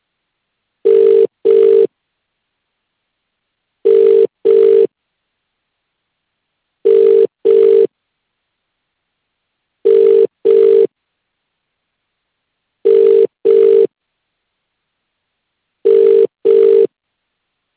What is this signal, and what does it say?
Ringback tone (UK), 400 + 450 Hz.